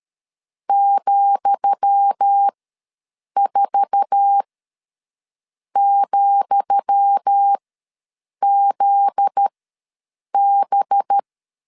Morse code, heard as a keyed tone, about 13 wpm, spaced as for 7 wpm, 800 Hz. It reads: ,4,ZB
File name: LEZIONI / 10 SIGNS / 10D COMA ws 12wpm 3618